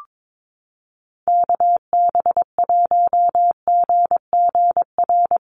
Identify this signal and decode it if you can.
Morse code, heard as a keyed tone, about 22 wpm, 700 Hz. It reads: KB1GGR
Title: USA Call Signs Repeated 3x 22wpm